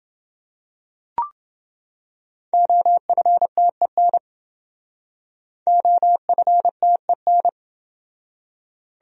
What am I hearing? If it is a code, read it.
Morse code, heard as a keyed tone, about 28 wpm, 700 Hz. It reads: OFTEN OFTEN